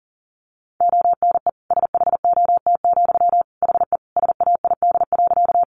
Morse code, heard as a keyed tone, about 40 wpm, 700 Hz. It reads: ONE SHOT, HE SAID.
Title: Sentences from Top 1100 Words 40wpm